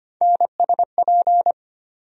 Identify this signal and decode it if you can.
Morse code, heard as a keyed tone, about 25 wpm, 700 Hz. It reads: NSP
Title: USA Call Sign Postfixes - 3 Letters 25wpm